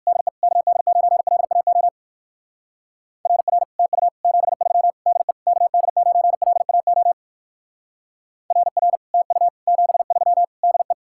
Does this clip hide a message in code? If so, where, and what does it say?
Yes: Morse code from 0.1 to 10.9 s, heard as a keyed tone, about 59 wpm, 700 Hz. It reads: DE KD9LAO RR TU 73 DE KD9LAO RR TU 73 DE